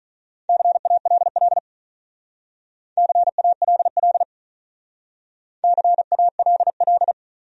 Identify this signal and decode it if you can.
Morse code, heard as a keyed tone, about 46 wpm, 700 Hz. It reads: CALL CALL CALL